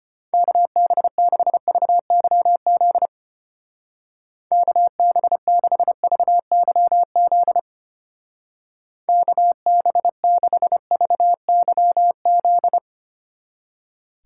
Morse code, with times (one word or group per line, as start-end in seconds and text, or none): 0.3-3.1 KB6VYZ
4.5-7.6 KB6VYZ
9.1-12.8 KB6VYZ